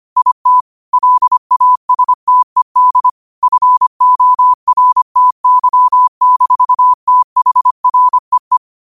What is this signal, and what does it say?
Morse code, heard as a keyed tone, about 25 wpm, 1000 Hz. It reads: IT LASTED FORTY-THREE
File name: qotdfiles / 25wpm